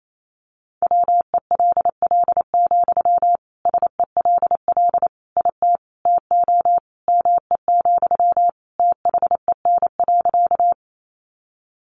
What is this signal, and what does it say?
Morse code, heard as a keyed tone, about 28 wpm, 700 Hz. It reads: WELL, SELL IT TO ME, THEN.
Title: Sentences from Top 1100 Words 28wpm